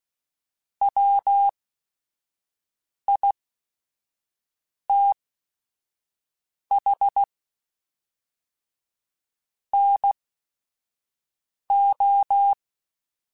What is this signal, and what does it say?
Morse code, heard as a keyed tone, about 16 wpm, spaced as for 5 wpm, 800 Hz. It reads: WITH NO
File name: Weather Sets / Weather Sets-mp3s / Weather 3 17 @5